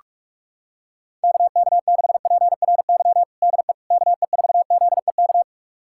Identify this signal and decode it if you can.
Morse code, heard as a keyed tone, about 45 wpm, 690 Hz. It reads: KK6PRY DE KE4ZEK
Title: USA Callsign DE Callsign - 6 Characters - 45wpm